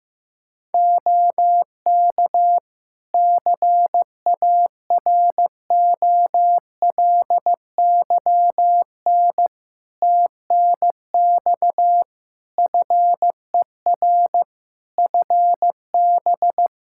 Morse code, heard as a keyed tone, about 16 wpm, 700 Hz. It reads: OK CAROLYN TNX FER FB
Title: Practice QSO - Exchange 4 - Ending - 15wpm